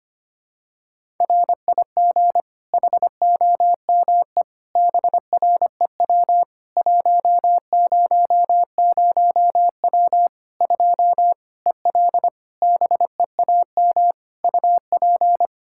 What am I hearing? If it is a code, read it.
Morse code, heard as a keyed tone, about 25 wpm, 700 Hz. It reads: RIG HOME BREW 100W 2 EL BEAM UP
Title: Practice QSO - Exchange 2 - Info - 25wpm